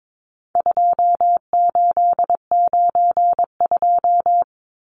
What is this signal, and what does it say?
Morse code, heard as a keyed tone, about 22 wpm, 700 Hz. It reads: 2892
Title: Numbers - 4-Digit 22wpm